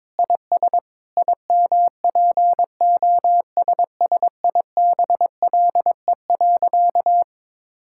Morse code, heard as a keyed tone, about 22 wpm, 700 Hz. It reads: IS IMPOSSIBLE.
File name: practice / 22wpm / Sentences with 4-Word Phrases - 22wpm